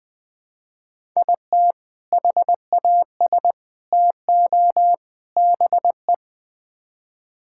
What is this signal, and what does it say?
Morse code, heard as a keyed tone, about 20 wpm, 700 Hz. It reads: IT HAS TO BE
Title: Build A Phrase - 4-Word Phrases - 20wpm